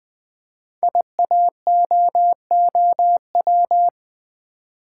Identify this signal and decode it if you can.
Morse code, heard as a keyed tone, about 20 wpm, 700 Hz. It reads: IAOOW
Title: Code Groups - 5 Letters - 20wpm